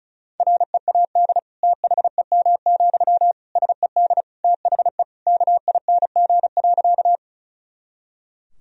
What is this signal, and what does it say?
Morse code, heard as a keyed tone, about 34 wpm, 700 Hz. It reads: READ THEM, SED THE KING.
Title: Sentences from Top 400 Words and Abbreviations 35wpm